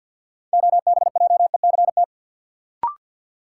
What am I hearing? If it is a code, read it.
Morse code, heard as a keyed tone, about 51 wpm, 700 Hz. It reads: OBJECT
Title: Sentences from Top 500 Words - New Word Drill - 50wpm